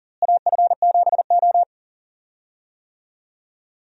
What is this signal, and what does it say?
Morse code, heard as a keyed tone, about 40 wpm, 700 Hz. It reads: AF7O
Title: USA Call Signs - 4 Characters - Repeated 2x - 40wpm